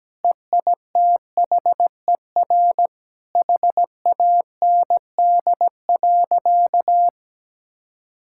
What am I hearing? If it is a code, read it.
Morse code, heard as a keyed tone, about 17 wpm, 700 Hz. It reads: EITHER HAND.
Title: Sentences from Top 1600 Words 17wpm